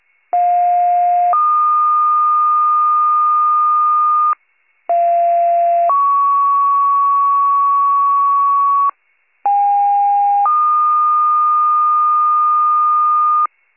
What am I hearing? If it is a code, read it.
Two-tone page, heard as a one-second tone, then a three-second tone, about 710 then 1220 Hz, about 690 then 1120 Hz, about 810 then 1220 Hz. It is heard from 0.3 to 13.5 s.